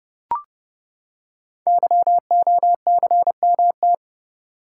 Morse code, heard as a keyed tone, about 30 wpm, 700 Hz. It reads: YOCMT
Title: Code Groups - 5 Letters - 30wpm